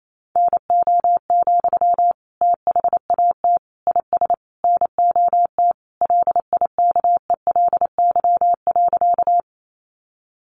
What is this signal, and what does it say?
Morse code, heard as a keyed tone, about 28 wpm, 700 Hz. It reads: NO, THAT IS NOT LIKELY.
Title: Sentences from Top 600 Words 28wpm